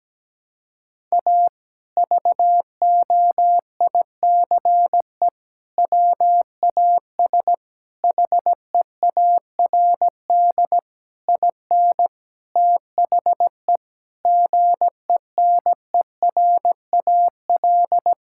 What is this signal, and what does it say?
Morse code, heard as a keyed tone, about 17 wpm, 700 Hz. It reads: A VOICE WAS HEARD IN THE GENERAL
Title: Sentences from Top 1000 Words - Review Part 5 - 17wpm